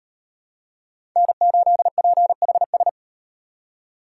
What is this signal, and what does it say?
Morse code, heard as a keyed tone, about 36 wpm, 700 Hz. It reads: N8PHS